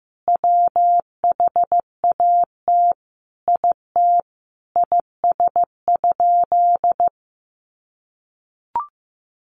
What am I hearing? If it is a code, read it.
Morse code, heard as a keyed tone, about 15 wpm, 700 Hz. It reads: WHAT IT IS?
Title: Sentences from Top 100 Words 15wpm